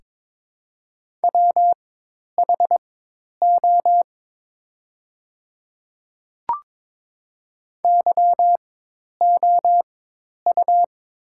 Morse code, heard as a keyed tone, about 22 wpm, spaced as for 10 wpm, 690 Hz. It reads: WHO YOU
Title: 4x Character Spacing - Top 100 Words - 22wpm